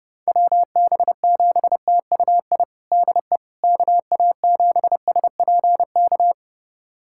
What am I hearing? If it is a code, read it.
Morse code, heard as a keyed tone, about 30 wpm, 700 Hz. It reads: WB7TUI DE KA7SPK
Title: USA Callsign DE Callsign 30wpm